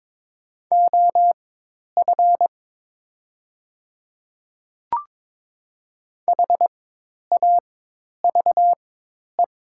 Morse code, heard as a keyed tone, about 22 wpm, spaced as for 10 wpm, 700 Hz. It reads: OF HAVE
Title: 4x Character Spacing - Top 100 Words - 22wpm